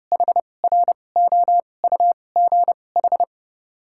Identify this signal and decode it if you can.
Morse code, heard as a keyed tone, about 30 wpm, 700 Hz. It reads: H R O U G H
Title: 2x Character Spacing - Top 200 Words - 30wpm